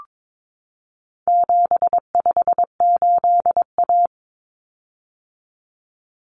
Morse code, heard as a keyed tone, about 22 wpm, 700 Hz. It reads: 758A
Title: Common Cut Numbers - 4-Digit 22wpm